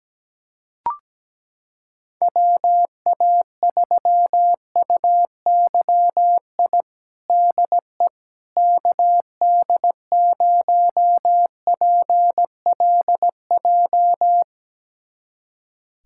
Morse code, heard as a keyed tone, about 17 wpm, 700 Hz. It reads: WA3UYI DE KD0PLJ
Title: USA Callsign DE Callsign - 6 Characters - 17wpm